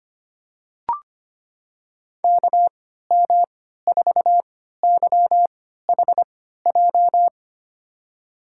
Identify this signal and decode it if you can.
Morse code, heard as a keyed tone, about 25 wpm, spaced as for 14 wpm, 700 Hz. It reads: KM4YHJ
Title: USA Call Signs - 3x Character Spacing - 25wpm